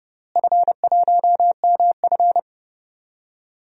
Morse code, heard as a keyed tone, about 30 wpm, 700 Hz. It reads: F1MF
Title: DXCC-100 Call Signs - 4 Characters - Repeated 2x - 30wpm